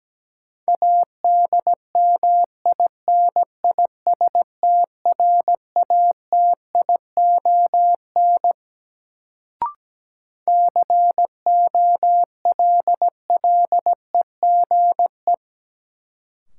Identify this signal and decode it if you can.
Morse code, heard as a keyed tone, about 17 wpm, 700 Hz. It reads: ADMINISTRATION COLLEGE
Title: Sentences from Top 1300 Words - New Word Drill - 17wpm